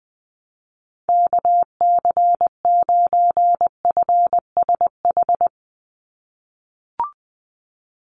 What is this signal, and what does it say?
Morse code, heard as a keyed tone, about 20 wpm, 700 Hz. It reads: KC9FSH
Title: USA Call Signs 20wpm